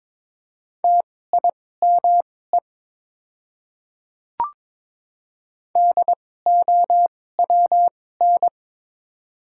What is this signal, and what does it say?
Morse code, heard as a keyed tone, about 22 wpm, spaced as for 16 wpm, 700 Hz. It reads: TIME DOWN
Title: 2x Character Spacing - Top 100 Words - 22wpm